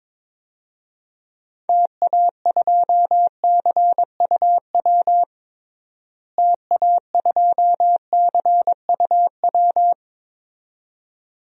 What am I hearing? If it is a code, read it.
Morse code, heard as a keyed tone, about 21 wpm, 700 Hz. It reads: TA2CUW TA2CUW
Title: DXCC-100 Call Signs - 6 Characters - Repeated 2x - 22wpm